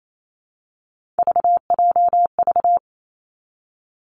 Morse code, heard as a keyed tone, about 28 wpm, 700 Hz. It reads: VJV
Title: Three Challenging Letter-Number Combinations 28wpm